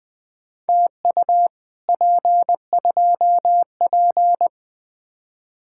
Morse code, heard as a keyed tone, about 20 wpm, 700 Hz. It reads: TU P2P
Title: Speed-Racing Common QSO Phrases - 17wpm